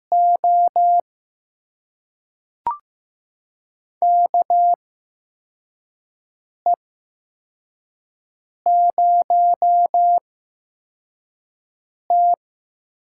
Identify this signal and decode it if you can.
Morse code, heard as a keyed tone, about 15 wpm, spaced as for 4 wpm, 700 Hz. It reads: O KE0T
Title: USA Call Signs - 6 Characters - 8x Character Spacing - 15wpm